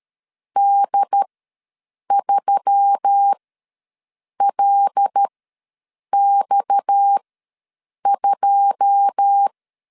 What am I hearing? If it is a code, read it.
Morse code, heard as a keyed tone, about 13 wpm, spaced as for 7 wpm, 800 Hz. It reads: D3LX2
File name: LEZIONI / 9 286 ws 12wpm 3351